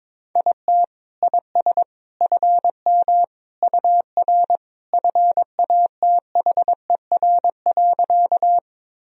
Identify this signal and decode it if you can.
Morse code, heard as a keyed tone, about 21 wpm, 700 Hz. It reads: IT IS FM UR FATHER.
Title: Sentences from Top 300 Words and Abbreviations 22wpm